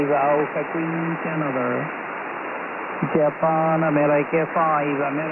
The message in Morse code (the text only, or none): none